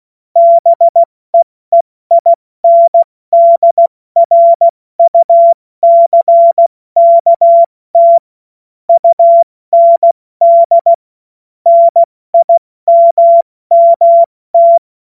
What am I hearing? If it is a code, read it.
Morse code, heard as a keyed tone, about 16 wpm, 680 Hz. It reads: BEEINDRUCKT UND NIMMT
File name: Morsekurs / Witze / Witz 19. Oktober 2021, 70 F80